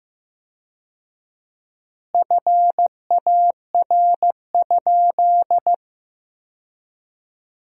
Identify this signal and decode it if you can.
Morse code, heard as a keyed tone, about 15 wpm, 700 Hz. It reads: FAR?
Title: 4x Word-Spacing - Sentences from Top 300 Words - 15wpm